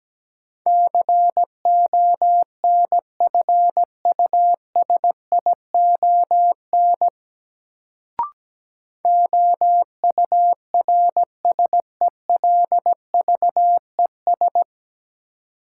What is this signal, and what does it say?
Morse code, heard as a keyed tone, about 17 wpm, 700 Hz. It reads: CONFUSION OURSELVES
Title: Sentences from Top 1600 Words - New Word Drill - 17wpm